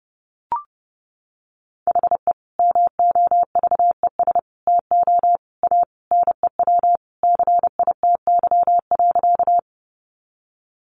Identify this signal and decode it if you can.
Morse code, heard as a keyed tone, about 30 wpm, 700 Hz. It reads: HE MOVES TO A NEW CITY.
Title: Easy Phrases 30wpm